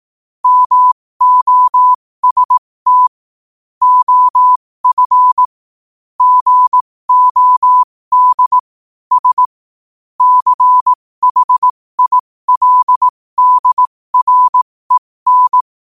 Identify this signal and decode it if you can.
Morse code, heard as a keyed tone, about 18 wpm, spaced as for 15 wpm, 1000 Hz. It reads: MOST OF GOD S CHILDREN
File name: qotdfiles / 15wpm